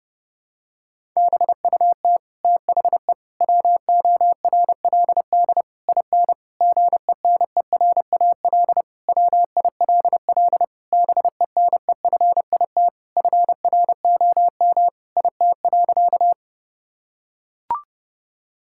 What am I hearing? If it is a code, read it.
Morse code, heard as a keyed tone, about 30 wpm, 700 Hz. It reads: BUT THE WORLD IN GENERAL WILL BENEFIT FROM IT.
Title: Sentences from Top 1000 Words - Review Part 6 - 30wpm